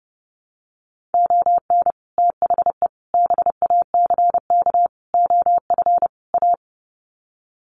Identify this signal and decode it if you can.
Morse code, heard as a keyed tone, about 30 wpm, 700 Hz. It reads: ON THE BACK OF A
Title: Sets of 5 Words Encore2 30wpm